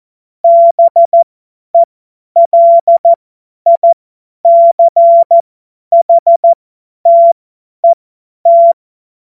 Morse code, heard as a keyed tone, about 14 wpm, spaced as for 10 wpm, 680 Hz. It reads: BELICHTET
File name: Witz-07-Oktober-2025-50-F70